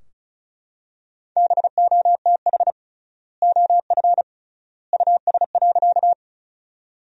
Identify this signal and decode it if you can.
Morse code, heard as a keyed tone, about 36 wpm, 700 Hz. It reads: BOTH OF US.
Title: 3x Word-Spacing - Sentences from Top 300 Words - 35wpm